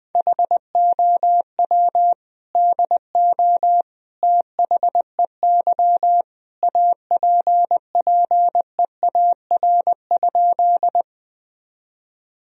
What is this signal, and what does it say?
Morse code, heard as a keyed tone, about 20 wpm, 700 Hz. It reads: HOW DO THEY APPEAR?
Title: Sentences from Top 1000 Words - Review Part 2 - 20wpm